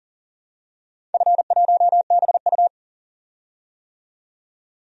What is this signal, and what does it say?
Morse code, heard as a keyed tone, about 40 wpm, 700 Hz. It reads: F1BU
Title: DXCC-100 Call Signs - 4 Characters - 40wpm